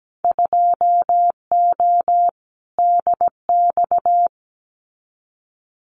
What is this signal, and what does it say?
Morse code, heard as a keyed tone, about 17 wpm, 700 Hz. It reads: 2O DX
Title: Winter Field Day Exchanges Encore 17wpm